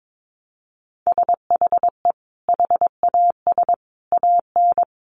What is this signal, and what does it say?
Morse code, heard as a keyed tone, about 22 wpm, 700 Hz. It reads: SHE HAS AN
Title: Sentences from Top 800 Words 22wpm